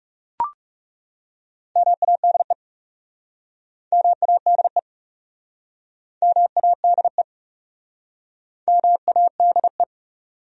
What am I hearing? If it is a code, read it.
Morse code, heard as a keyed tone, about 42 wpm, 700 Hz. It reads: MADE MADE MADE MADE